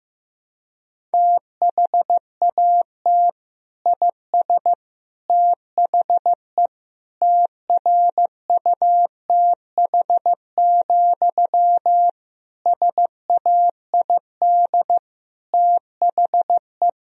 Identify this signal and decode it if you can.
Morse code, heard as a keyed tone, about 15 wpm, 700 Hz. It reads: THAT IS THE TRUTH, SAID THE
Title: Sentences from Top 1000 Words - Review Part 1 - 15wpm